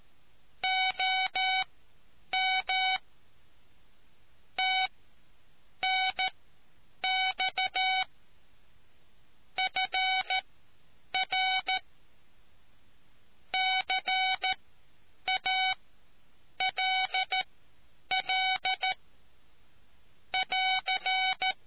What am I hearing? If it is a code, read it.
Morse code, heard as a keyed tone, about 13 wpm, spaced as for 8 wpm, 750 Hz. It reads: OM TNX FR CALL <AR>